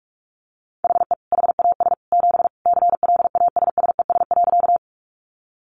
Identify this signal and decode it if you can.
Morse code, heard as a keyed tone, about 44 wpm, 700 Hz. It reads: HE HAS 7 CLASSES.
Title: Additional Easy Phrases 45wpm